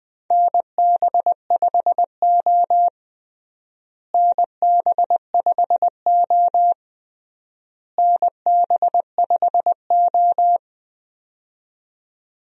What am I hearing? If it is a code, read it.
Morse code, heard as a keyed tone, about 20 wpm, 700 Hz. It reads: NB5O NB5O NB5O
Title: USA Call Signs - 4 Characters - Repeated 3x - 20wpm